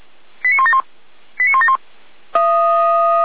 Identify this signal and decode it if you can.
Five-tone selective call (ZVEI), heard as quick runs of five tones, tones of about 80 ms. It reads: 89171 89171